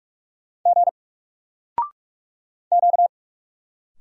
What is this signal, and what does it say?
Morse code, heard as a keyed tone, about 46 wpm, 700 Hz. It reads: G Q
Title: Single Character (Letter:Number:Punctuation:Prosign) 45wpm